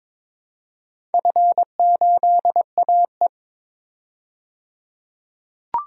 Morse code, heard as a keyed tone, about 22 wpm, 700 Hz. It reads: F8AE
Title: DXCC-100 Call Signs - 4 Characters - 22wpm